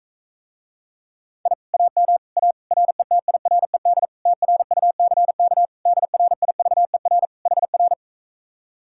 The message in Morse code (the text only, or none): I AM A RETIRED TRUCK DRIVER HR